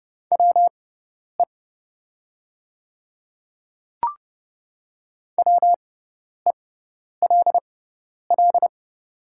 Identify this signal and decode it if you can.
Morse code, heard as a keyed tone, about 30 wpm, spaced as for 10 wpm, 700 Hz. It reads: WE WELL
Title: 6x Character Spacing - Top 300 Words - 30wpm